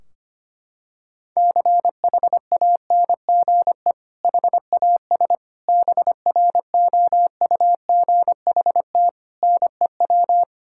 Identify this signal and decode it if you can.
Morse code, heard as a keyed tone, about 25 wpm, 700 Hz. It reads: CHANGE HAS BROUGHT NEW
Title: Sentences from Top 1600 Words 25wpm